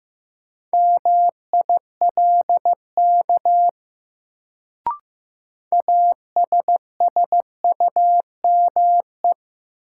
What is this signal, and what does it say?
Morse code, heard as a keyed tone, about 15 wpm, 700 Hz. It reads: MILK ASSUME
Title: Sentences from Top 1500 Words - New Word Drill - 15wpm